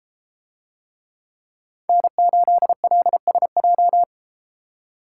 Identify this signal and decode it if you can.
Morse code, heard as a keyed tone, about 32 wpm, 700 Hz. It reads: N8LSJ